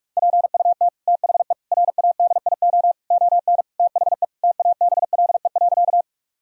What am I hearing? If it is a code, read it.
Morse code, heard as a keyed tone, about 47 wpm, 700 Hz. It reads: PUT THE RADIO ON THE TABLE.
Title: Sentences from Top 1000 Words - Review Part 3 - 45wpm